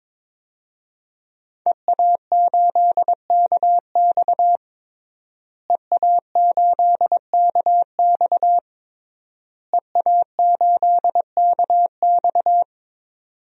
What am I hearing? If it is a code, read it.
Morse code, heard as a keyed tone, about 21 wpm, 700 Hz. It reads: EA8KX EA8KX EA8KX
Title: DXCC-100 Call Signs - 5 Characters - Repeated 3x - 22wpm